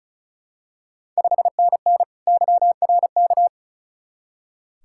Morse code, heard as a keyed tone, about 36 wpm, 690 Hz. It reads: 5NN YRK
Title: Ontario QSO Party Exchange 35wpm